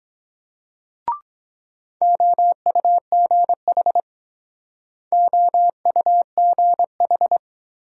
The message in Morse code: OUGH OUGH

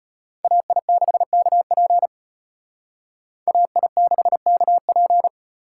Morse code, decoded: AI6KP AI6KP

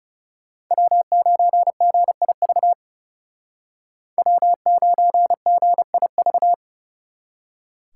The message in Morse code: W9GIV W9GIV

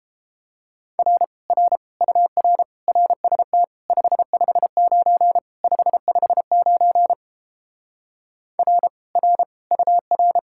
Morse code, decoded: R R UR RST 559 559 R R UR

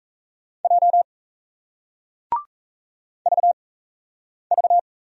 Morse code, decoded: J VV

